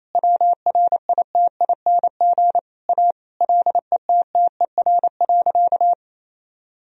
Morse code, heard as a keyed tone, about 28 wpm, 700 Hz. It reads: WRITING A LETTER.
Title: Sentences from Top 1000 Words - Review Part 4 - 28wpm